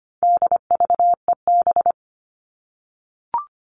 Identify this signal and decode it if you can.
Morse code, heard as a keyed tone, about 25 wpm, 700 Hz. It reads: DVEB
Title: All Cut Numbers - 4-Digit 25wpm